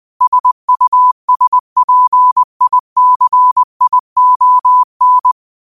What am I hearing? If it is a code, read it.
Morse code, heard as a keyed tone, about 20 wpm, 1000 Hz. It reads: SUSPICION
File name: qotdfiles / 20wpm